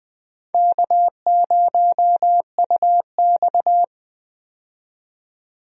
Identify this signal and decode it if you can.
Morse code, heard as a keyed tone, about 20 wpm, 700 Hz. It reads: K0UX
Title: USA Call Signs - 4 Characters - 20wpm